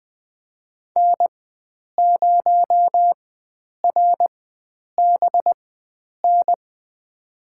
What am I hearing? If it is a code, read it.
Morse code, heard as a keyed tone, about 20 wpm, spaced as for 9 wpm, 700 Hz. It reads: N0RBN
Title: USA Call Signs - 5 Characters - 4x Character Spacing - 20wpm